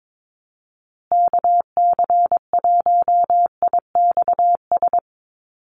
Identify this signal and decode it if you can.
Morse code, heard as a keyed tone, about 22 wpm, 700 Hz. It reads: KC1IXS
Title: USA Call Signs Repeated 3x 22wpm